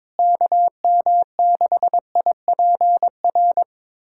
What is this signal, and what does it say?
Morse code, heard as a keyed tone, about 22 wpm, 700 Hz. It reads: KM6IPR
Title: USA Callsign DE Callsign - 6 Characters - 22wpm